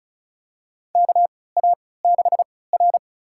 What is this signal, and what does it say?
Morse code, heard as a keyed tone, about 36 wpm, spaced as for 20 wpm, 700 Hz. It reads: KA6R
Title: USA Call Signs - 4 Characters - 3x Character Spacing - 35wpm